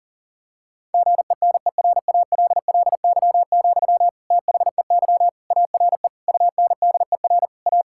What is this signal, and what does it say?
Morse code, heard as a keyed tone, about 37 wpm, 700 Hz. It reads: GENERALLY, THEY ARE UNDER A
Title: Sentences from Top 1000 Words - Review Part 6 - 40wpm